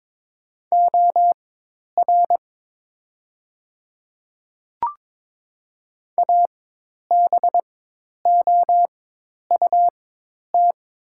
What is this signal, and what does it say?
Morse code, heard as a keyed tone, about 22 wpm, spaced as for 10 wpm, 700 Hz. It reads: OR ABOUT
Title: 4x Character Spacing - Top 100 Words - 22wpm